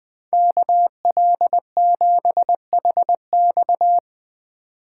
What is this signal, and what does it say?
Morse code, heard as a keyed tone, about 20 wpm, 700 Hz. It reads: KL7HX
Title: Speed Racing - USA 2x2 Call Signs 17wpm